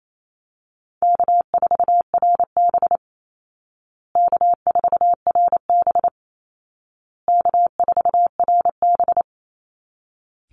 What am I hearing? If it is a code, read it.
Morse code, heard as a keyed tone, about 28 wpm, 700 Hz. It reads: K4RB K4RB K4RB
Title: Easy Call Signs Repeated 3x - 28wpm